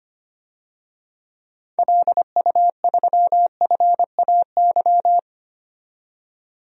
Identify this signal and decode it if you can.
Morse code, heard as a keyed tone, about 25 wpm, 700 Hz. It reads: LU3FAY
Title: DXCC-100 Call Signs - 6 Characters - 25wpm